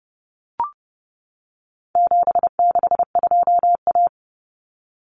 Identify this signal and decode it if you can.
Morse code, heard as a keyed tone, about 30 wpm, 700 Hz. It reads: 762A
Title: Common Cut Numbers - 4-Digit 30wpm